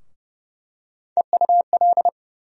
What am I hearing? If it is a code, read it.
Morse code, heard as a keyed tone, about 29 wpm, 700 Hz. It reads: EUL
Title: USA Call Sign Postfixes - 3 Letters 30wpm